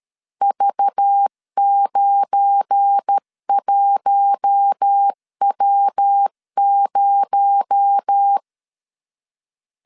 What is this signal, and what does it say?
Morse code, heard as a keyed tone, about 13 wpm, 800 Hz. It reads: V91W0
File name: LEZIONI / 8 YJ45 / 8 YJ45 12wpm 2151